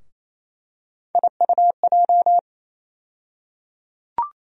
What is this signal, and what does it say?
Morse code, heard as a keyed tone, about 27 wpm, 700 Hz. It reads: IUJ
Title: USA Call Sign Postfixes - 3 Letters 28wpm